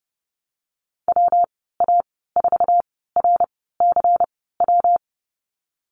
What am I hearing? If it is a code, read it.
Morse code, heard as a keyed tone, about 30 wpm, spaced as for 17 wpm, 700 Hz. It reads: WA4RCW
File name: Lesson 19g - Letter w - Call Signs 30wpm